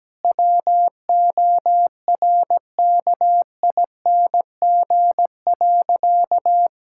Morse code, heard as a keyed tone, about 17 wpm, 700 Hz. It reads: WORKING.
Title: Sentences from Top 1000 Words - Review Part 6 - 17wpm